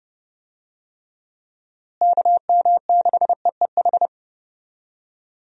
Morse code, heard as a keyed tone, about 32 wpm, 700 Hz. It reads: KM6EEH
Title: USA Call Signs - 6 Characters - Repeated 2x - 30wpm